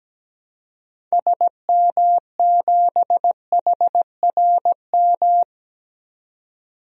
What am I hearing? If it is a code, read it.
Morse code, heard as a keyed tone, about 17 wpm, 700 Hz. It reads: SM7HRM
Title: DXCC-100 Call Signs - 6 Characters - 17wpm